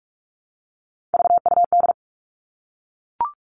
Morse code, heard as a keyed tone, about 44 wpm, 700 Hz. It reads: VUD
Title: All Cut Numbers - 3-Digit 45wpm